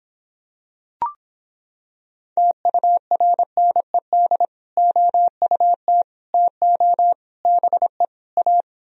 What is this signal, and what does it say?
Morse code, heard as a keyed tone, about 26 wpm, 700 Hz. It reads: TURNED OUT TO BE A